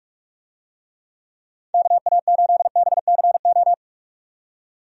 Morse code, heard as a keyed tone, about 46 wpm, 700 Hz. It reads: KA8BCO
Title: USA Call Signs - 6 Characters - 45wpm